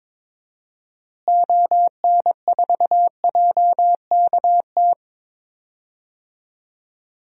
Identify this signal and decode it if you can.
Morse code, heard as a keyed tone, about 22 wpm, 700 Hz. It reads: ON4JKT
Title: DXCC-100 Call Signs - 6 Characters - Repeated 2x - 22wpm